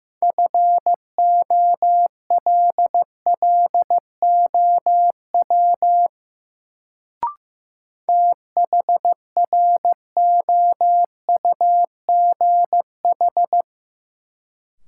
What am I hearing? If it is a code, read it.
Morse code, heard as a keyed tone, about 15 wpm, 700 Hz. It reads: FOLLOW THROUGH
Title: Sentences from Top 200 Words - New Word Drill - 15wpm